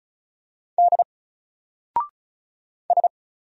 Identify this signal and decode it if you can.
Morse code, heard as a keyed tone, about 36 wpm, 700 Hz. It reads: D S
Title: Single Character (Letter:Number:Punctuation:Prosign) 35wpm